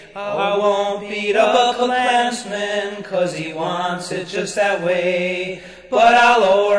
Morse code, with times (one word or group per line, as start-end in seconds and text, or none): none